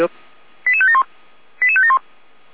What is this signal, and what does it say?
Five-tone selective call (ZVEI), heard as quick runs of five tones, tones of about 70 ms, with speech. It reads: 88571 88571